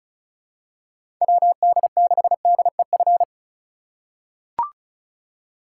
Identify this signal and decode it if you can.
Morse code, heard as a keyed tone, about 36 wpm, 700 Hz. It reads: WD6DEF
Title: USA Call Signs - 6 Characters - 2x Character Spacing - 35wpm